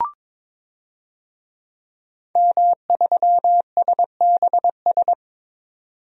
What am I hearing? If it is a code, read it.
Morse code, heard as a keyed tone, about 22 wpm, 700 Hz. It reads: M3SBS
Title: DXCC-100 Call Signs - 5 Characters - 22wpm